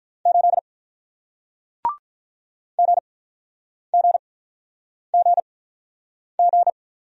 Morse code, heard as a keyed tone, about 47 wpm, 700 Hz. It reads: 8 G G G G